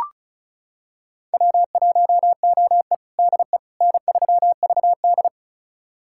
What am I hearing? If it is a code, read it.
Morse code, heard as a keyed tone, about 36 wpm, 700 Hz. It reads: W1OE DE N3VD
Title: USA Callsign DE Callsign - 4 Characters - 35wpm